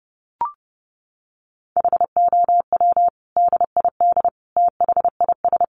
Morse code, heard as a keyed tone, about 30 wpm, 700 Hz. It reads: HOW DID THIS
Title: Sentences from Top 400 Words 30wpm